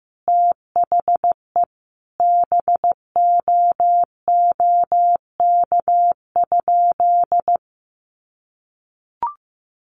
Morse code, heard as a keyed tone, about 15 wpm, 700 Hz. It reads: THE BOOK?
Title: Sentences from Top 1300 Words 15wpm